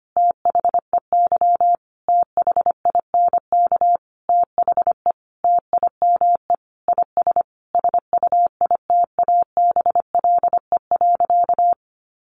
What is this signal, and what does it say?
Morse code, heard as a keyed tone, about 25 wpm, 700 Hz. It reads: THEY THINK THE TIME IS SUITABLE.
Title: Sentences from Top 1400 Words 25wpm